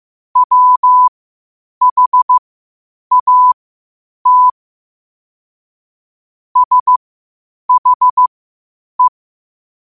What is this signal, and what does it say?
Morse code, heard as a keyed tone, about 15 wpm, spaced as for 8 wpm, 1000 Hz. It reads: WHAT SHE
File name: qotdfiles / 8wpm